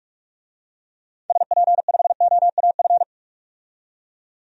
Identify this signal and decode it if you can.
Morse code, heard as a keyed tone, about 46 wpm, 700 Hz. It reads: SP5OAF
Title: DXCC-100 Call Signs - 6 Characters - 45wpm